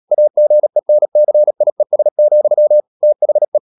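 Morse code, heard as a keyed tone, about 37 wpm, 600 Hz. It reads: AGENCIES, THE
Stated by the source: Morse code practice at 35–39 wpm sending English text of Ken Follett's novel Never, from CW med Gnister.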